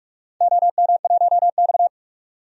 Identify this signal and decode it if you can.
Morse code, heard as a keyed tone, about 47 wpm, 700 Hz. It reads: OM1X
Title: DXCC-100 Call Signs - 4 Characters - 45wpm